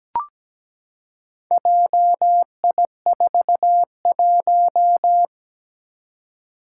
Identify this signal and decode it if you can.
Morse code, heard as a keyed tone, about 17 wpm, 700 Hz. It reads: JI41
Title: Four Challenging Letter-Number Combinations 17wpm